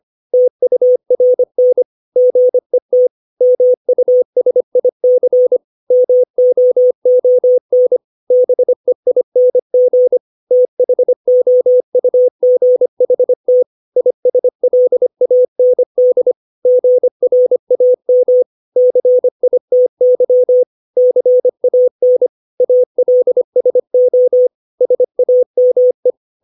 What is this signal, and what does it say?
Morse code, heard as a keyed tone, about 25 wpm, 500 Hz. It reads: TURN GET MUSIC MOON BEING THOUGHT ISLAND GRAM CITY CAN ALSO SAME
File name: 632 words / 632 Words 25WPM